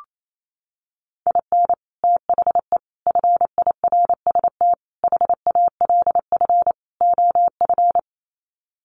Morse code, heard as a keyed tone, about 28 wpm, 700 Hz. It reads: IN THE FIRST HALF OF
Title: Sets of 5 Words 28wpm